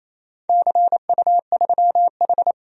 Morse code, heard as a keyed tone, about 27 wpm, 700 Hz. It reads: CU3H
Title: DXCC-100 Call Signs - 4 Characters - 28wpm